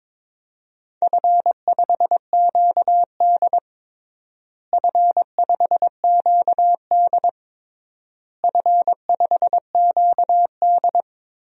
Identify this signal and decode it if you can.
Morse code, heard as a keyed tone, about 22 wpm, 700 Hz. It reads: F5QD F5QD F5QD
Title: DXCC-100 Call Signs - 4 Characters - Repeated 3x - 22wpm